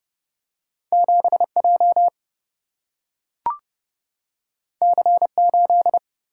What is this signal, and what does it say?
Morse code, heard as a keyed tone, about 29 wpm, 700 Hz. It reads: 7J C8
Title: Single-Letter Single-Number 30wpm